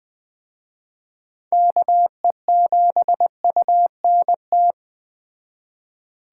Morse code, heard as a keyed tone, about 21 wpm, 700 Hz. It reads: KE7UNT